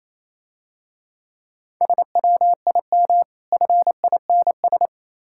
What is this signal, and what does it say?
Morse code, heard as a keyed tone, about 27 wpm, 690 Hz. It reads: SWIM FINS
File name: Easy Words - Speed-Racing with Hints - 22wpm